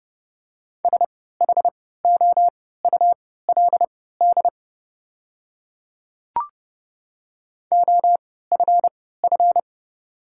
Morse code, heard as a keyed tone, about 30 wpm, spaced as for 17 wpm, 700 Hz. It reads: SHOULD OFF
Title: 3x Character Spacing - Top 500 Words - 30wpm